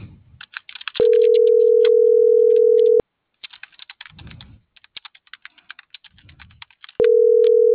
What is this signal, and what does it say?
Ringback tone (North America), 440 + 480 Hz.